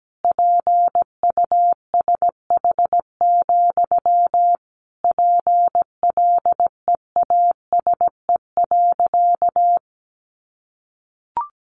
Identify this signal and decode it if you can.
Morse code, heard as a keyed tone, about 17 wpm, 700 Hz. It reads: PUSH, PLEASE.
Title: Sentences from Top 1500 Words 17wpm